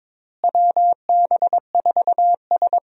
Morse code, heard as a keyed tone, about 22 wpm, 700 Hz. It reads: WB4S
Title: USA Call Signs - 4 Characters - Repeated 2x - 22wpm